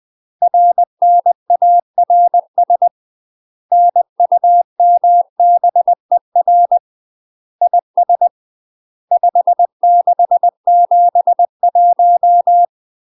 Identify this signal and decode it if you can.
Morse code, heard as a keyed tone, about 20 wpm, 700 Hz. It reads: RNARS NUMBER IS 5671
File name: UK MASTER QSOs TRAINING / 20WPM / UK MASTER QSOs 0038